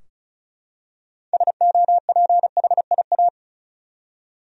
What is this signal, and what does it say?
Morse code, heard as a keyed tone, about 36 wpm, 700 Hz. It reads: SOPHIA
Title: USA Common Names - 35wpm